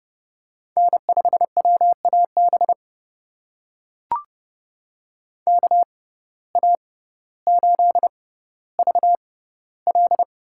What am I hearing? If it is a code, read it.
Morse code, heard as a keyed tone, about 30 wpm, spaced as for 15 wpm, 700 Hz. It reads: N5WAB KA8VL
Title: USA Call Signs - 6x Character Spacing - 30wpm